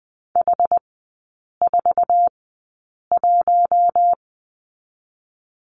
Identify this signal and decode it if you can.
Morse code, heard as a keyed tone, about 20 wpm, spaced as for 8 wpm, 700 Hz. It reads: H41
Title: 093 Lesson 16e - Letter c - Three Characters 20wpm